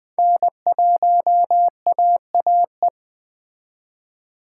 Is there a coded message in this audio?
Yes: Morse code from 0.2 to 2.9 s, heard as a keyed tone, about 20 wpm, 700 Hz. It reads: N1AAE